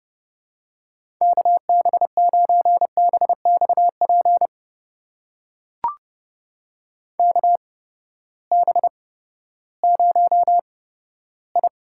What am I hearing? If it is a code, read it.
Morse code, heard as a keyed tone, about 30 wpm, spaced as for 15 wpm, 700 Hz. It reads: KB9BXP KB0I
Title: USA Call Signs - 6 Characters - 8x Character Spacing - 30wpm